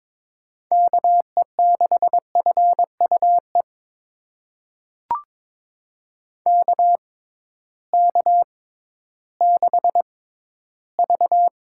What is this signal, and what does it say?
Morse code, heard as a keyed tone, about 21 wpm, spaced as for 13 wpm, 700 Hz. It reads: KE6FUE KK6V
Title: USA Call Signs - 6 Characters - 6x Character Spacing - 22wpm